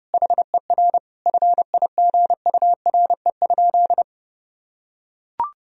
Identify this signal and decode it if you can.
Morse code, heard as a keyed tone, about 30 wpm, 700 Hz. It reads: HER FIGURE?
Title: Sentences from Top 1000 Words - Review Part 2 - 30wpm